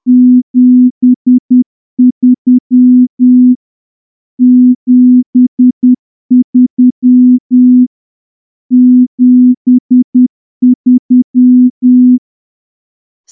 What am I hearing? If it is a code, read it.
Morse code, heard as a keyed tone, about 10 wpm, 250 Hz. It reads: 73 73 73
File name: morse-training-10WPM